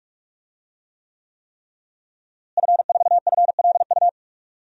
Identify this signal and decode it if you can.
Morse code, heard as a keyed tone, about 44 wpm, 700 Hz. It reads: F4FLU
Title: DXCC-100 Call Signs - 5 Characters - Repeated 3x - 45wpm